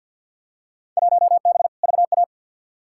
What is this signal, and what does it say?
Morse code, heard as a keyed tone, about 52 wpm, 700 Hz. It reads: 1B VA
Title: Field Day Exchanges - 50wpm